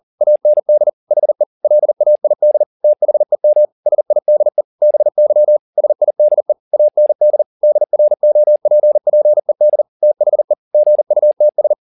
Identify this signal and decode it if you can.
Morse code, heard as a keyed tone, about 40 wpm, 600 Hz. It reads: AND HE LAID THEM SIDE BY SIDE AND DROPPED THE GUTS
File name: the old man and the sea 2 40wpm